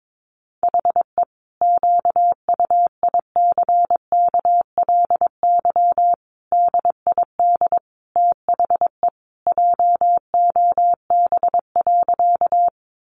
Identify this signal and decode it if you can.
Morse code, heard as a keyed tone, about 22 wpm, 700 Hz. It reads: HE QUICKLY DID THE JOB.
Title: Sentences from Top 700 Words 22wpm